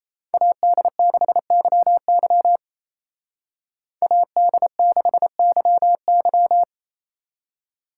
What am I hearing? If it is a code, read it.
Morse code, heard as a keyed tone, about 36 wpm, 700 Hz. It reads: AD6YY AD6YY